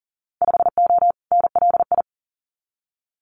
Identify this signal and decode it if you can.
Morse code, heard as a keyed tone, about 40 wpm, 700 Hz. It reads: 5O NLI
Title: Winter Field Day Exchanges 40wpm